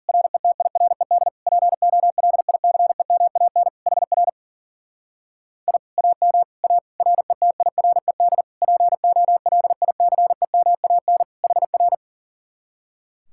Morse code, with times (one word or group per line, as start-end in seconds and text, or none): 0.1-1.3 RETIRED
1.5-3.7 POLICEMAN
3.9-4.3 HR
5.7-5.8 I
6.0-6.4 AM
6.6-6.8 A
7.0-8.4 RETIRED
8.6-11.2 POLICEMAN
11.4-11.9 HR